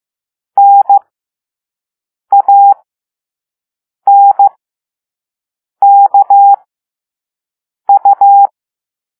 Morse code, heard as a keyed tone, about 15 wpm, spaced as for 5 wpm, 800 Hz. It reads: NANKU
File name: Morse / Level5-All-B